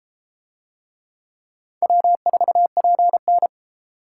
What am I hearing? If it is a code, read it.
Morse code, heard as a keyed tone, about 32 wpm, 700 Hz. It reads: W4PN